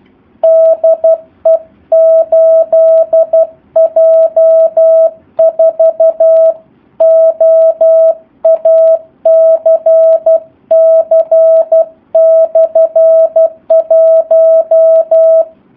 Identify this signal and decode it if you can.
Morse code, heard as a keyed tone, about 12 wpm, 650 Hz. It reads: DE 8J4OACC/1
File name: jh8cbh / 2019cw / SP04- w